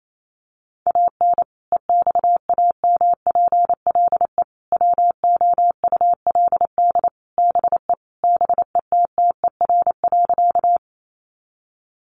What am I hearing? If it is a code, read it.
Morse code, heard as a keyed tone, about 28 wpm, 700 Hz. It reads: AN EXAMPLE WOULD BE BETTER.
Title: Sentences from Top 400 Words 28wpm